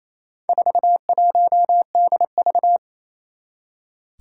Morse code, heard as a keyed tone, about 28 wpm, 700 Hz. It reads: 41DV
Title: Four Challenging Letter-Number Combinations 28wpm